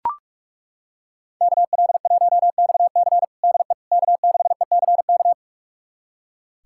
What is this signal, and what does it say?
Morse code, heard as a keyed tone, about 41 wpm, 700 Hz. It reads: KL1XC DE K6ECK
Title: USA Callsign DE Callsign - 5 Characters - 45wpm